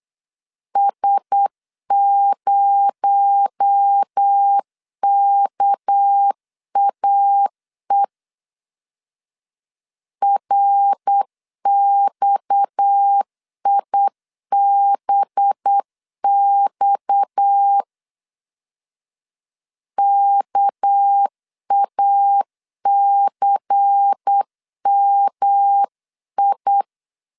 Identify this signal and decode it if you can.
Morse code, heard as a keyed tone, about 8 wpm, 800 Hz. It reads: S0KAE RXIBX KACMI
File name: CW / DET / 1 6 1500C S B 8wpm 5142